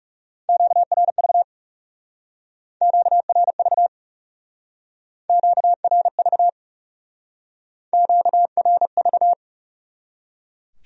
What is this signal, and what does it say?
Morse code, heard as a keyed tone, about 36 wpm, 700 Hz. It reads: QRV QRV QRV QRV
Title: Speed-Racing Q-Codes - 30wpm